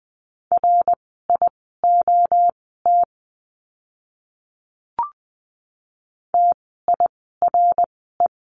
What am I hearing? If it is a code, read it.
Morse code, heard as a keyed tone, about 20 wpm, spaced as for 15 wpm, 700 Hz. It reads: RIOT TIRE
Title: 061 Lesson 11f - Letter h - Words 20wpm